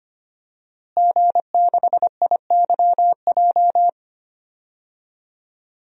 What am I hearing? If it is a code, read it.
Morse code, heard as a keyed tone, about 25 wpm, 700 Hz. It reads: G6IYJ
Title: DXCC-100 Call Signs - 5 Characters - Repeated 3x - 25wpm